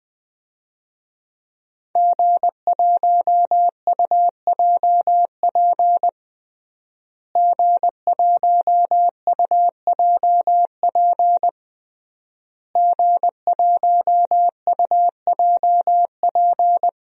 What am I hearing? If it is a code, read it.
Morse code, heard as a keyed tone, about 21 wpm, 700 Hz. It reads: G1UJP G1UJP G1UJP
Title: DXCC-100 Call Signs - 5 Characters - Repeated 3x - 20wpm